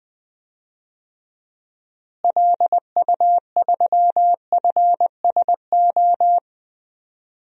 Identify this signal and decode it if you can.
Morse code, heard as a keyed tone, about 21 wpm, 700 Hz. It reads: LU3FSO
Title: DXCC-100 Call Signs - 6 Characters - Repeated 2x - 20wpm